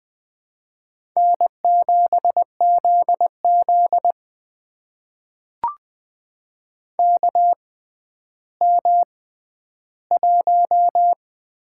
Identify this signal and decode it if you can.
Morse code, heard as a keyed tone, about 20 wpm, spaced as for 11 wpm, 700 Hz. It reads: N7ZZ KM1
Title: USA Call Signs - 4 Characters - 6x Character Spacing - 20wpm